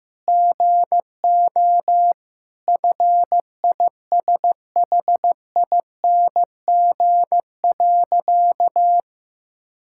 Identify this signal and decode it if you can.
Morse code, heard as a keyed tone, about 15 wpm, 700 Hz. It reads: GO FISHING.
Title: Sentences from Top 1000 Words - Review Part 6 - 15wpm